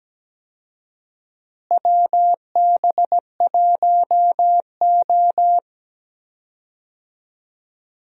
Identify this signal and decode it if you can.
Morse code, heard as a keyed tone, about 17 wpm, 700 Hz. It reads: WB1O
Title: USA Call Signs - 4 Characters - Repeated 3x - 17wpm